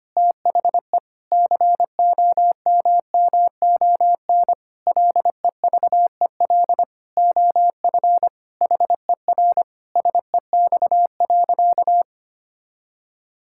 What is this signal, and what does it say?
Morse code, heard as a keyed tone, about 25 wpm, 700 Hz. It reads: THE COMMON LEVEL OF HER SEX.
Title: Sentences from Top 1600 Words 25wpm